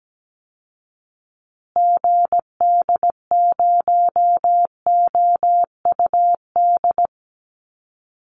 Morse code, heard as a keyed tone, about 17 wpm, 700 Hz. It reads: GD0OUD
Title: Call Signs Repeated 2x - 17wpm